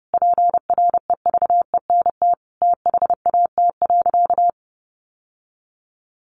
Morse code, heard as a keyed tone, about 30 wpm, 700 Hz. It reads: PREVENT THAT.
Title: Sentences from Top 1100 Words 30wpm